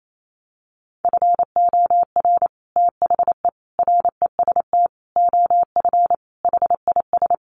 Morse code, heard as a keyed tone, about 28 wpm, 700 Hz. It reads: FOR THE REST OF HIS
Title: Sets of 5 Words 28wpm